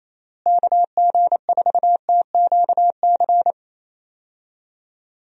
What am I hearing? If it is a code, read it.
Morse code, heard as a keyed tone, about 27 wpm, 700 Hz. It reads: KG4TQC
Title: USA Call Signs - 6 Characters - Repeated 2x - 28wpm